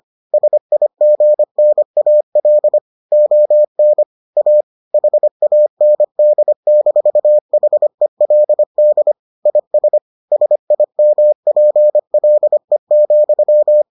Morse code, heard as a keyed tone, about 25 wpm, 600 Hz. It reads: SIGNAL ON A HAND-HELD IS SIMPLE,